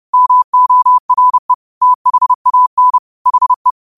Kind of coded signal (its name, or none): Morse code